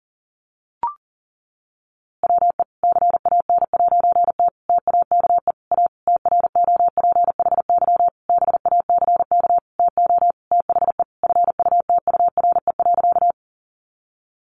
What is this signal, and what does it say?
Morse code, heard as a keyed tone, about 40 wpm, 700 Hz. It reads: WE CAN'T TAKE A TROPHY BACK TO THE FUTURE.
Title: A Sound Of Thunder 40wpm